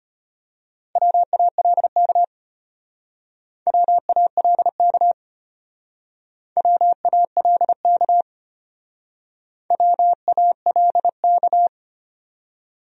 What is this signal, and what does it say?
Morse code, heard as a keyed tone, about 31 wpm, 700 Hz. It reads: WALK WALK WALK WALK
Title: Speed-Racing Top 500 Words 25wpm